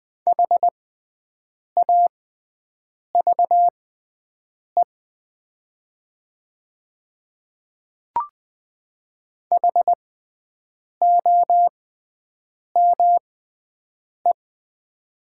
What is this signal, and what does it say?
Morse code, heard as a keyed tone, about 20 wpm, spaced as for 7 wpm, 700 Hz. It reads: HAVE HOME
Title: 6x Character Spacing - Top 200 Words - 20wpm